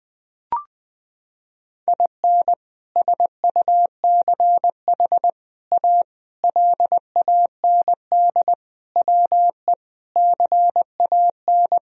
Morse code, heard as a keyed tone, about 20 wpm, 700 Hz. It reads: IN SUCH A LAND WE CAN
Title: Sentences from Top 1000 Words - Review Part 1 - 20wpm